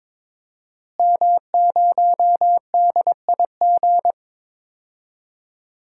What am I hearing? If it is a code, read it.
Morse code, heard as a keyed tone, about 22 wpm, 700 Hz. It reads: M0DIG
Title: DXCC-100 Call Signs - 5 Characters - Repeated 3x - 22wpm